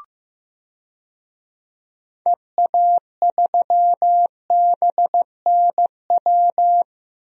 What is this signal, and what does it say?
Morse code, heard as a keyed tone, about 15 wpm, 700 Hz. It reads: EA3BNW
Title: DXCC-100 Call Signs - 6 Characters - 15wpm